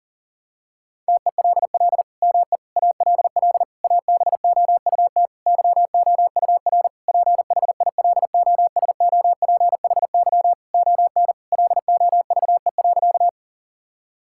Morse code, heard as a keyed tone, about 40 wpm, 690 Hz. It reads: TELL ME ALL ABOUT YOUR PHILOSOPHY ON LOVE.
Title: Sentences from Top 1000 Words - Review Part 6 - 40wpm